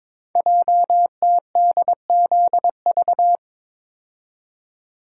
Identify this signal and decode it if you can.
Morse code, heard as a keyed tone, about 22 wpm, 700 Hz. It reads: JTDZV
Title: Code Groups - 5 Letters - 22wpm